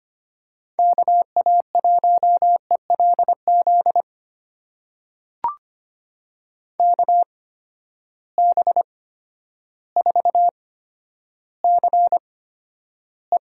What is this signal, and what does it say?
Morse code, heard as a keyed tone, about 25 wpm, spaced as for 11 wpm, 700 Hz. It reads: KA1ELZ KB4CE
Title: USA Call Signs - 8x Character Spacing - 25wpm